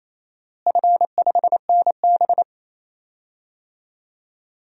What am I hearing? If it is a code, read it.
Morse code, heard as a keyed tone, about 28 wpm, 700 Hz. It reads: F5NB
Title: DXCC-100 Call Signs - 4 Characters - Repeated 2x - 28wpm